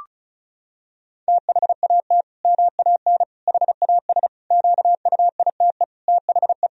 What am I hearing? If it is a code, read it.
Morse code, heard as a keyed tone, about 36 wpm, 700 Hz. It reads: THAT MAN HAS QUITE THE
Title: Sentences from Top 1000 Words - Review Part 1 - 35wpm